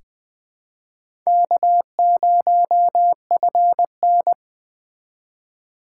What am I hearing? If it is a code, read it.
Morse code, heard as a keyed tone, about 20 wpm, 700 Hz. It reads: K0FN
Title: USA Call Signs - 3x Character Spacing - 20wpm